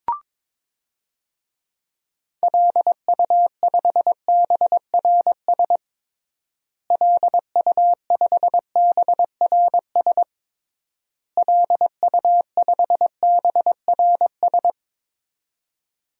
Morse code, heard as a keyed tone, about 22 wpm, 700 Hz. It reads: LU5BRS LU5BRS LU5BRS
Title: DXCC-100 Call Signs - 6 Characters - Repeated 3x - 22wpm